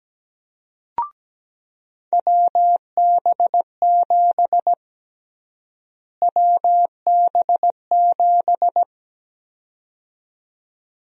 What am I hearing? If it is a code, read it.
Morse code, heard as a keyed tone, about 17 wpm, 700 Hz. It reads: WB7 WB7
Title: USA Call Sign Prefixes - Repeated 2x - 17wpm